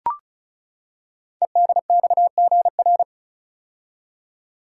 Morse code, heard as a keyed tone, about 35 wpm, 690 Hz. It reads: EDXGR
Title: Code Groups - 5 Letters - 35wpm